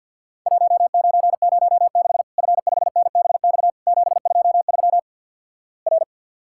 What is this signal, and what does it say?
Morse code, heard as a keyed tone, about 49 wpm, 700 Hz. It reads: 1906 F5NBX 723 R
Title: Worked All Europe DX Contest - QTC Exchange - 50wpm